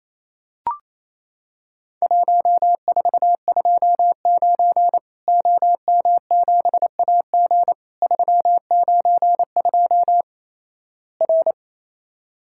Morse code, heard as a keyed tone, about 28 wpm, 700 Hz. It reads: 1429 OM7AG 392 R
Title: Worked All Europe DX Contest - QTC Exchange - 28wpm